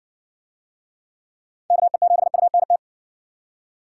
Morse code, heard as a keyed tone, about 60 wpm, 700 Hz. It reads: KE7UNT